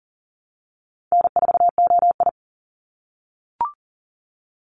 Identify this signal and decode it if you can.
Morse code, heard as a keyed tone, about 37 wpm, 700 Hz. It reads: N4OI
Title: Call Signs - 40wpm